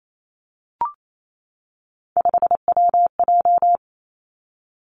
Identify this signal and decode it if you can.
Morse code, heard as a keyed tone, about 28 wpm, 700 Hz. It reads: 5WJ
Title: Three Challenging Letter-Number Combinations 28wpm